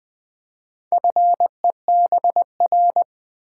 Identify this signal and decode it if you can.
Morse code, heard as a keyed tone, about 20 wpm, 700 Hz. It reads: FEBR
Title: Build A Word - 2 Letters At A Time - 8-Letter Words - 20wpm